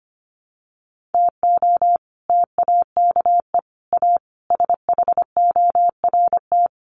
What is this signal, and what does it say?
Morse code, heard as a keyed tone, about 25 wpm, 700 Hz. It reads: TO TAKE A SHORT
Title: Sets of 5 Words Encore2 25wpm